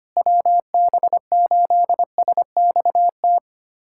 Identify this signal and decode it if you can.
Morse code, heard as a keyed tone, about 25 wpm, 700 Hz. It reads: WB8SXT
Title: USA Callsign DE Callsign - 6 Characters - 25wpm